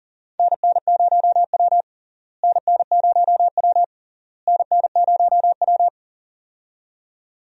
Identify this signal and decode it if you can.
Morse code, heard as a keyed tone, about 40 wpm, 700 Hz. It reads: NN0W NN0W NN0W
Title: USA Call Signs - 4 Characters - Repeated 3x - 40wpm